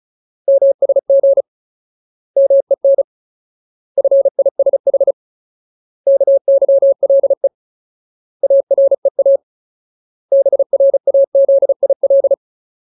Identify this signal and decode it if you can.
Morse code, heard as a keyed tone, about 35 wpm, 550 Hz. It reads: MSG MEN FISH KYLE AREA BRAZIL
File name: Ultimate Word List / 35WPM Ultimate